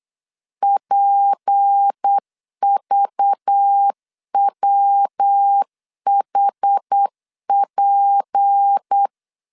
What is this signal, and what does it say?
Morse code, heard as a keyed tone, about 8 wpm, 800 Hz. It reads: PVWHP
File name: LEZIONI / 6 PKX0 / 6 P 8wpm 3025